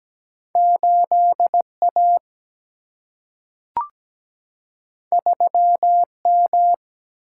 Morse code, heard as a keyed tone, about 17 wpm, 700 Hz. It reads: 8 A 3 M
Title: Single-Letter Single-Number 17wpm